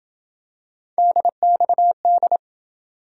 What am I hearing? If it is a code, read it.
Morse code, heard as a keyed tone, about 26 wpm, 700 Hz. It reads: DXD